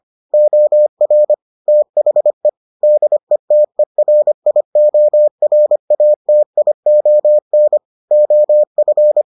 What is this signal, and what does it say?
Morse code, heard as a keyed tone, about 25 wpm, 600 Hz. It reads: OR THE DETERIORATION OF